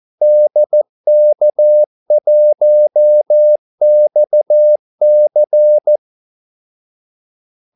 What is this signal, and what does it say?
Morse code, heard as a keyed tone, about 14 wpm, 600 Hz. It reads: DK1XC
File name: Callsigns-14wpm